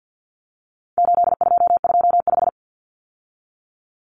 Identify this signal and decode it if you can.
Morse code, heard as a keyed tone, about 50 wpm, 700 Hz. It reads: 8225
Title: Numbers - 4-Digit 50wpm